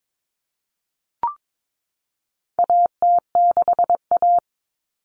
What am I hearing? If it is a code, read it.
Morse code, heard as a keyed tone, about 22 wpm, 700 Hz. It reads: AT6A
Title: Common Cut Numbers - 4-Digit 22wpm